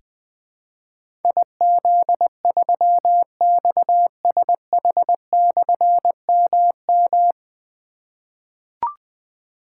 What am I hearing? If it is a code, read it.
Morse code, heard as a keyed tone, about 20 wpm, 700 Hz. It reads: IZ3XSH/MM
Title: Difficult Call Signs 20wpm